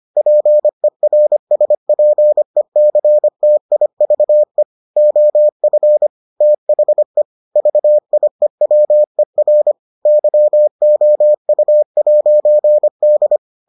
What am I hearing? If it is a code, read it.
Morse code, heard as a keyed tone, about 25 wpm, 600 Hz. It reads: PERSPECTIVE OF THE VIEWER YOU'D